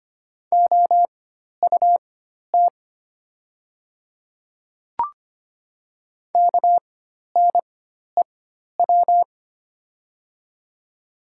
Morse code, heard as a keyed tone, about 25 wpm, spaced as for 12 wpm, 700 Hz. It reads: OUT KNEW
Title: 4x Character Spacing - Top 500 Words - 25wpm